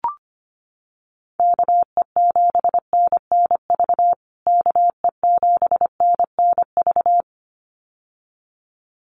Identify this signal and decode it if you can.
Morse code, heard as a keyed tone, about 25 wpm, 700 Hz. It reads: KE7NNV KE7NNV
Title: USA Call Signs Repeated 2x 25wpm